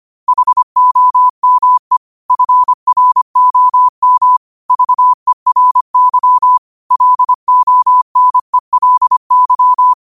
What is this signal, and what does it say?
Morse code, heard as a keyed tone, about 25 wpm, 1000 Hz. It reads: HOME FROM VERY LONELY